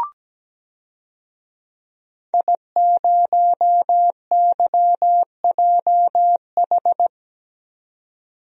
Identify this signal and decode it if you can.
Morse code, heard as a keyed tone, about 17 wpm, 700 Hz. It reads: I0YJH
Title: DXCC-100 Call Signs - 5 Characters - 17wpm